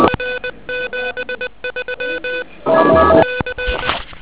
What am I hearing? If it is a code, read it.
Morse code, heard as a keyed tone, about 20 wpm, 1500 Hz, with noise. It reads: R 73 C
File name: OK QRP 2007 / REC10